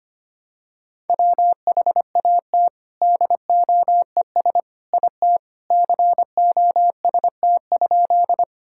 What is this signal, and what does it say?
Morse code, heard as a keyed tone, about 25 wpm, 700 Hz. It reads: WHAT DOES IT COST?
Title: Sentences from Top 1000 Words - Review Part 5 - 25wpm